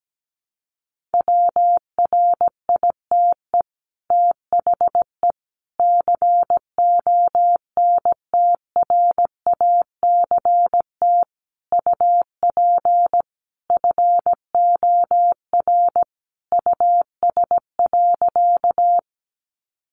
Morse code, heard as a keyed tone, about 17 wpm, 700 Hz. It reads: WRITE THE CONTRACT UP FOR US.
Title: Sentences from Top 1400 Words 17wpm